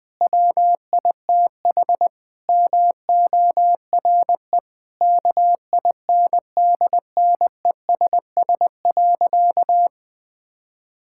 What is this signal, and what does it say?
Morse code, heard as a keyed tone, about 20 wpm, 700 Hz. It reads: WITH MORE KINDNESS.
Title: Sentences with 7-Word Phrases - 20wpm